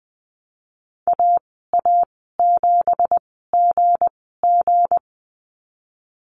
Morse code, heard as a keyed tone, about 19 wpm, spaced as for 14 wpm, 700 Hz. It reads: AA7GG
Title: 193 Lesson 31g - Letter v - Call Signs 20wpm